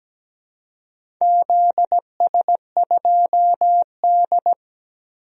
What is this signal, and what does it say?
Morse code, heard as a keyed tone, about 17 wpm, 700 Hz. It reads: ZS2D
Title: DXCC-100 Call Signs - 4 Characters - Repeated 2x - 17wpm